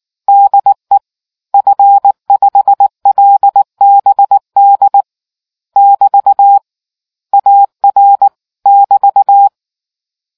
Morse code, heard as a keyed tone, about 19 wpm, 800 Hz. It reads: DE F5LBD <BT> AR <BT>